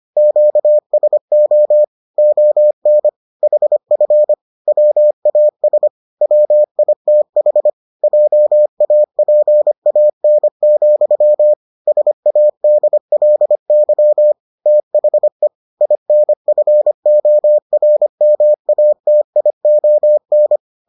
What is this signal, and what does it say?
Morse code, heard as a keyed tone, about 25 wpm, 600 Hz. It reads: QSO ON HF WAS WITH JAPAN, SADLY THE INFORMATION